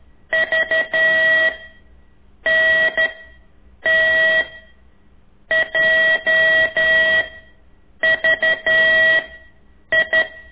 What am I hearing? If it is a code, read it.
Morse code, heard as a keyed tone, about 11 wpm, 1950 Hz, with noise. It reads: V N T J V I
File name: ik1weg2 / CORSI CW